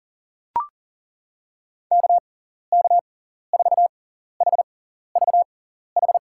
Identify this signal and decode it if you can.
Morse code, heard as a keyed tone, about 40 wpm, spaced as for 14 wpm, 700 Hz. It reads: KK4HVH
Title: USA Call Signs - 6 Characters - 6x Character Spacing - 40wpm